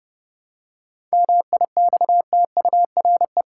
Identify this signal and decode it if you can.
Morse code, heard as a keyed tone, about 30 wpm, 700 Hz. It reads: MIXTURE
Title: Seven Letter Words - 30wpm